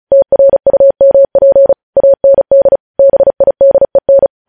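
Morse code, heard as a keyed tone, about 36 wpm, 550 Hz. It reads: TRUMP AND BIDEN
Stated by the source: Morse news headlines practice at 35 words per minute.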